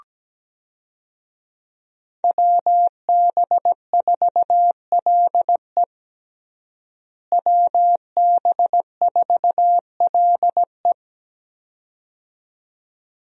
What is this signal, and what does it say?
Morse code, heard as a keyed tone, about 17 wpm, 700 Hz. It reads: WB4LE WB4LE
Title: USA Call Signs - 5 Characters - Repeated 2x - 17wpm